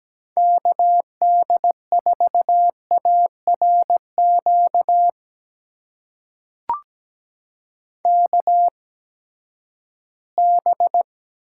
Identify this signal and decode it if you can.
Morse code, heard as a keyed tone, about 17 wpm, spaced as for 12 wpm, 700 Hz. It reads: KD4ARQ KB
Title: USA Call Signs - 6 Characters - 8x Character Spacing - 17wpm